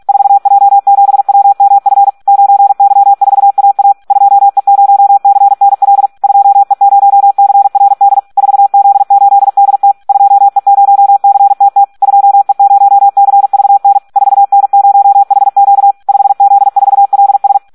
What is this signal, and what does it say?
Morse code, heard as a keyed tone, about 46 wpm, 800 Hz. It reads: 4J7WMF 9Y4AA 2E0CNL 2E0CRD 4Z8DT 2E0CTT 2E0CVN 4N0HQ 5Z4LS
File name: Audio practica CW / SD 1 46